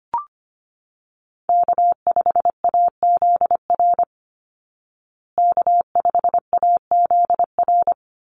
Morse code, heard as a keyed tone, about 25 wpm, 700 Hz. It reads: K5AZR K5AZR
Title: Call Signs Repeated 3x - 25wpm